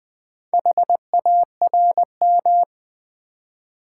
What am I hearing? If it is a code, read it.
Morse code, heard as a keyed tone, about 20 wpm, 700 Hz. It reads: HARM
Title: Build A Word - 1 Letter At A Time - 4-Letter Words - 20wpm